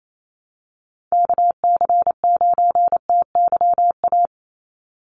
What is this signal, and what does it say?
Morse code, heard as a keyed tone, about 28 wpm, 700 Hz. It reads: KC9TYA
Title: NA SOTA Chaser Callsigns 28wpm